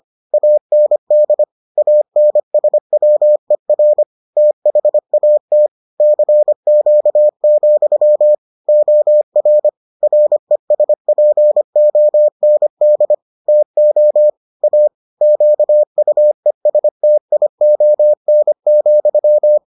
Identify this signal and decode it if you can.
Morse code, heard as a keyed tone, about 25 wpm, 600 Hz. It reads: AND ANSWER THAT CQ, OR RESPOND TO A QUESTION,